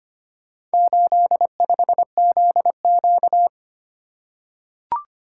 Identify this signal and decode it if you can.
Morse code, heard as a keyed tone, about 25 wpm, 700 Hz. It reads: 85ZQ
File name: practice / 25wpm / Four Letter-Number Combinations - 25wpm